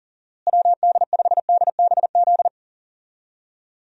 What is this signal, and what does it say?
Morse code, heard as a keyed tone, about 40 wpm, 700 Hz. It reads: WD5DBZ
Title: USA Call Signs - 6 Characters - Repeated 2x - 40wpm